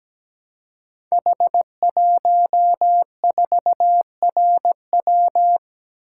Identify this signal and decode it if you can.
Morse code, heard as a keyed tone, about 17 wpm, 700 Hz. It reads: H14RW
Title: Code Groups - 5 Letter:Numbers - 17wpm